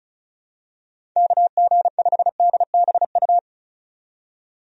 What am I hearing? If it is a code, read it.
Morse code, heard as a keyed tone, about 36 wpm, 700 Hz. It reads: KG5DBU
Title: USA Call Signs - 6 Characters - 35wpm